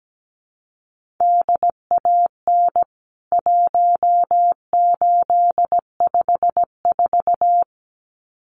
Morse code, heard as a keyed tone, about 17 wpm, 700 Hz. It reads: DAN 1854
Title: CWT Exchange 17wpm